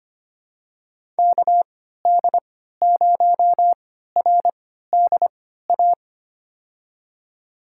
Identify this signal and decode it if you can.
Morse code, heard as a keyed tone, about 25 wpm, spaced as for 14 wpm, 700 Hz. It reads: KD0RDA
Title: USA Call Signs - 6 Characters - 3x Character Spacing - 25wpm